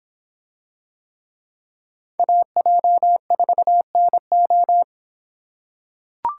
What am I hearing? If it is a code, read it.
Morse code, heard as a keyed tone, about 26 wpm, 700 Hz. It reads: AJ4NO